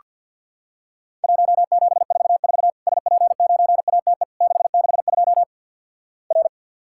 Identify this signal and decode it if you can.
Morse code, heard as a keyed tone, about 49 wpm, 700 Hz. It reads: 1744 SP9ATE 662 R
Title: Worked All Europe DX Contest - QTC Exchange - 50wpm